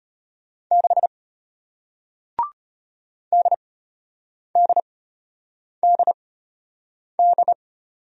Morse code, heard as a keyed tone, about 33 wpm, spaced as for 8 wpm, 700 Hz. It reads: 6 DDDD